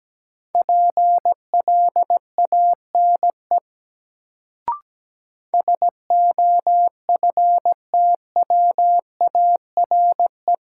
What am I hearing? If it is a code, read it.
Morse code, heard as a keyed tone, about 17 wpm, 700 Hz. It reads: PLANE SOFTWARE
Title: Sentences from Top 1000 Words - Review Part 3 - New Word Drill - 17wpm